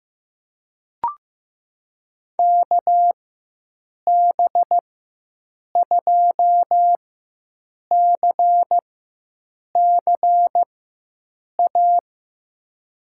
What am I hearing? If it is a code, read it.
Morse code, heard as a keyed tone, about 16 wpm, spaced as for 7 wpm, 700 Hz. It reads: KB2CCA
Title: USA Call Signs - 6 Characters - 4x Character Spacing - 15wpm